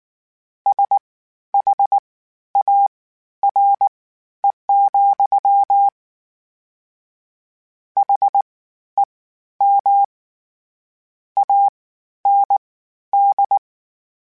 Morse code, heard as a keyed tone, about 19 wpm, spaced as for 11 wpm, 800 Hz. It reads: SHARE, HEM AND
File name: Misc / Misc-mp3s / WordPairs sp @18